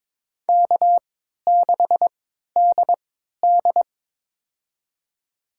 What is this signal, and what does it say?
Morse code, heard as a keyed tone, about 22 wpm, spaced as for 12 wpm, 700 Hz. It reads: K6DD
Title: USA Call Signs - 4 Characters - 3x Character Spacing - 22wpm